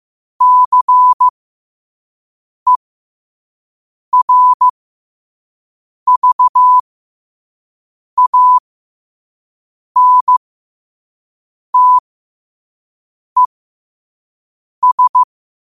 Morse code, heard as a keyed tone, about 15 wpm, spaced as for 5 wpm, 1000 Hz. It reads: CERVANTES
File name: qotdfiles / 5wpm